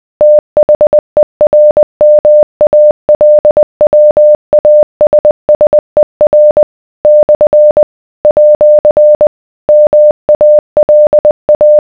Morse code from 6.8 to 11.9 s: / @ MALA